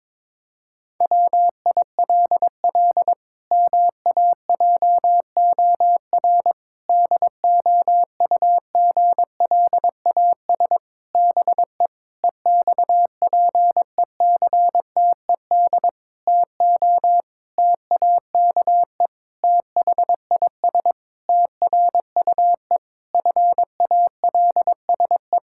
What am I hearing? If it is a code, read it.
Morse code, heard as a keyed tone, about 22 wpm, 700 Hz. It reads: WILL MAJOR DOUGLAS BE EXPECTED TO TAKE THIS TRUE FALSE
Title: Pangram Challenge v2 - 22wpm